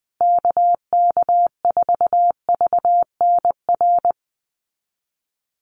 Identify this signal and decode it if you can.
Morse code, heard as a keyed tone, about 20 wpm, 700 Hz. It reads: KK4VNR
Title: USA Call Signs Repeated 2x 20wpm